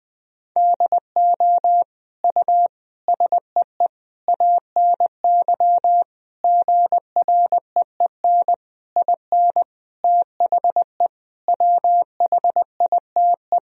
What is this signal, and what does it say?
Morse code, heard as a keyed tone, about 20 wpm, 700 Hz. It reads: DO U SEE ANY GREEN IN THE WHITE
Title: Sentences from Top 500 Words and Abbreviations 20wpm